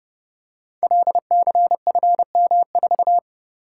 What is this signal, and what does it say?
Morse code, heard as a keyed tone, about 28 wpm, 700 Hz. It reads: LCFM4
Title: Code Groups - 5 Letter:Numbers - 30wpm